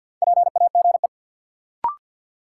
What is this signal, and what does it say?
Morse code, heard as a keyed tone, about 53 wpm, 700 Hz. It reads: PAGE
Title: Sets of 5 Words Encore - New Word Drill - 50wpm